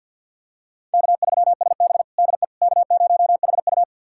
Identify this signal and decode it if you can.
Morse code, heard as a keyed tone, about 51 wpm, 700 Hz. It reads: K3SB DE K0HU
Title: USA Callsign DE Callsign - 4 Characters - 50wpm